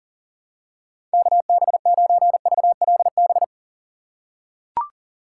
Morse code, heard as a keyed tone, about 40 wpm, 700 Hz. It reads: KB9VLB
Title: USA Call Signs - 6x Character Spacing - 40wpm